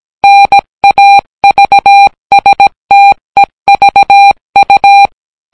Morse code, heard as a keyed tone, about 17 wpm, 800 Hz. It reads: NAVSTEVU